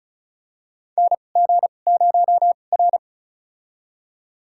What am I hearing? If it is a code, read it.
Morse code, heard as a keyed tone, about 36 wpm, spaced as for 25 wpm, 700 Hz. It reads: NG0R
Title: USA Call Signs - 2x Character Spacing - 35wpm